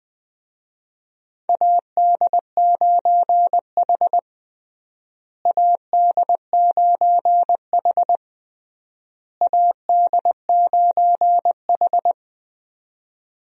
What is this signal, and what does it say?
Morse code, heard as a keyed tone, about 20 wpm, 700 Hz. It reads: AD9H AD9H AD9H
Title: USA Call Signs - 4 Characters - Repeated 3x - 20wpm